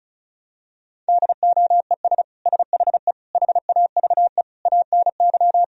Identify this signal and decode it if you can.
Morse code, heard as a keyed tone, about 34 wpm, 700 Hz. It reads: DOES SHE HAVE ANY
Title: Sentences from Top 1000 Words - Review Part 4 - 35wpm